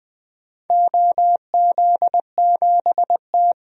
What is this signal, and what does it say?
Morse code, heard as a keyed tone, about 20 wpm, 700 Hz. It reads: OZ7T
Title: DXCC-100 Call Signs - 4 Characters - Repeated 3x - 20wpm